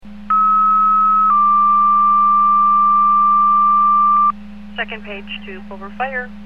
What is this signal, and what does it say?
Two-tone page, heard as a one-second tone, then a three-second tone, about 1290 then 1180 Hz, with speech. It is heard from 0.3 to 4.3 s.